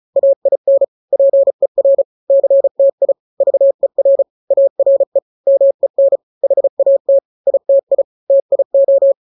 Morse code, heard as a keyed tone, about 34 wpm, 550 Hz. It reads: AIN PER CTI VER ARE MEN HAT ITI TIO
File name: 3-grams 35 30 WPM